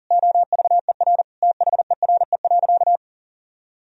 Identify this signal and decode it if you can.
Morse code, heard as a keyed tone, about 40 wpm, 700 Hz. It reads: OVER THERE.
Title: Sentences from Top 1000 Words - Review Part 2 - 40wpm